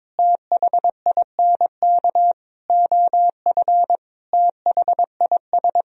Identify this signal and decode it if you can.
Morse code, heard as a keyed tone, about 22 wpm, 700 Hz. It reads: THINK OF THIS
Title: Sentences from Top 1000 Words - Review Part 2 - 22wpm